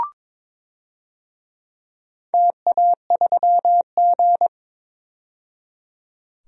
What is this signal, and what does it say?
Morse code, heard as a keyed tone, about 22 wpm, 700 Hz. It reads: TA3G
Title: DXCC-100 Call Signs - 4 Characters - 22wpm